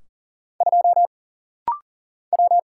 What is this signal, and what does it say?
Morse code, heard as a keyed tone, about 40 wpm, 700 Hz. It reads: 2 W
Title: Single Letter-Number - Rapid-Fire 40wpm